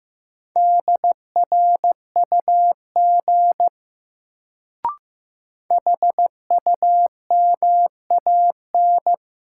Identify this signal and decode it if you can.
Morse code, heard as a keyed tone, about 15 wpm, 700 Hz. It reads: DRUG HUMAN
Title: Sets of 5 Words - New Word Drill - 15wpm